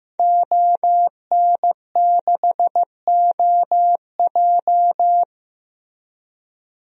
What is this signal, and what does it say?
Morse code, heard as a keyed tone, about 15 wpm, 700 Hz. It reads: ON6OJ
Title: DXCC-100 Call Signs - 5 Characters - Repeated 3x - 15wpm